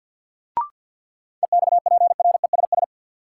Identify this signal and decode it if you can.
Morse code, heard as a keyed tone, about 52 wpm, 700 Hz. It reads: EXPRESS
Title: Sentences from Top 1100 Words - New Word Drill Alt - 50wpm